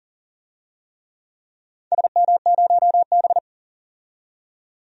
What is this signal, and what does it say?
Morse code, heard as a keyed tone, about 38 wpm, 700 Hz. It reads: SM0B
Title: DXCC-100 Call Signs - 4 Characters - 40wpm